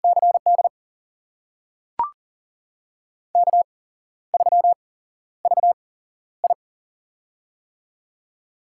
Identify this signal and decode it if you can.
Morse code, heard as a keyed tone, about 40 wpm, spaced as for 13 wpm, 700 Hz. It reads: CD K3VI
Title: USA Call Signs - 8x Character Spacing - 40wpm